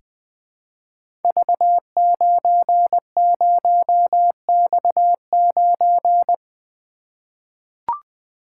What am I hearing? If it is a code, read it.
Morse code, heard as a keyed tone, about 20 wpm, 700 Hz. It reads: V90X9
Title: Code Groups - 5 Letter:Numbers - 20wpm